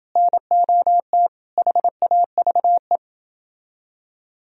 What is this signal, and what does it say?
Morse code, heard as a keyed tone, about 29 wpm, 700 Hz. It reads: NOT HAVE